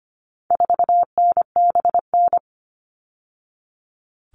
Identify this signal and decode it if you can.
Morse code, heard as a keyed tone, about 25 wpm, 700 Hz. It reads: 4NBN
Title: All Cut Numbers - 4-Digit 25wpm